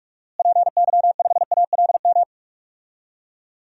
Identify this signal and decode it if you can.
Morse code, heard as a keyed tone, about 46 wpm, 700 Hz. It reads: PY5ALM
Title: DXCC-100 Call Signs - 6 Characters - Repeated 3x - 45wpm